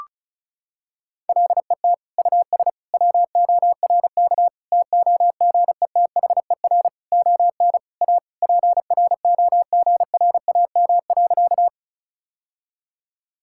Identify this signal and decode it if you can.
Morse code, heard as a keyed tone, about 36 wpm, 700 Hz. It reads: LET US WORK TOGETHER ON A PROGRAM.
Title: Sentences from Top 1000 Words - Review Part 5 - 35wpm